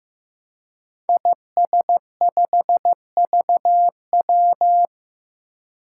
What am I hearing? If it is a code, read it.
Morse code, heard as a keyed tone, about 15 wpm, 700 Hz. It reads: IS5VW
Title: Five Challenging Letter-Number Combinations 15wpm